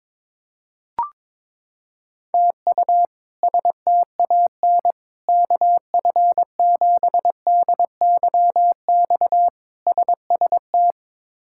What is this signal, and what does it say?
Morse code, heard as a keyed tone, about 22 wpm, 700 Hz. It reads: TU STAN KF7DYX SST
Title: K1USN SST QSO Practice - 22wpm